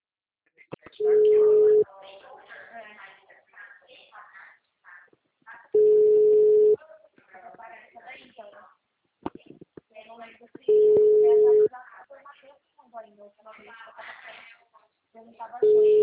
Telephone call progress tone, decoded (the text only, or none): ringback tone